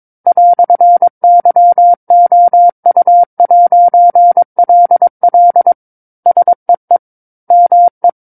Morse code, heard as a keyed tone, about 22 wpm, 700 Hz. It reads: "YOU'LL SEE ME
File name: 06-Chapter-06 22 WPM